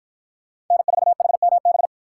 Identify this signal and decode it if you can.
Morse code, heard as a keyed tone, about 49 wpm, 700 Hz. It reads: N4HMB